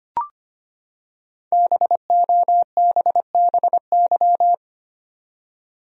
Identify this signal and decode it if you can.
Morse code, heard as a keyed tone, about 25 wpm, 700 Hz. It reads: BOBBY
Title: USA Common Names - Max 5 Letters - 25wpm